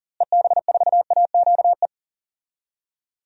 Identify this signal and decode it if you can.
Morse code, heard as a keyed tone, about 39 wpm, 700 Hz. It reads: EB4AQE
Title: DXCC-100 Call Signs - 6 Characters - Repeated 2x - 40wpm